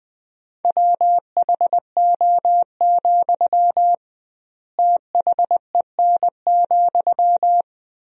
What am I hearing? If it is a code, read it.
Morse code, heard as a keyed tone, about 20 wpm, 700 Hz. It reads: WHO, THEN,
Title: 2x Word-Spacing - Sentences from Top 100 Words - 20wpm